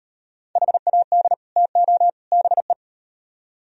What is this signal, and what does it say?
Morse code, heard as a keyed tone, about 38 wpm, 700 Hz. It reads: HAD TO BE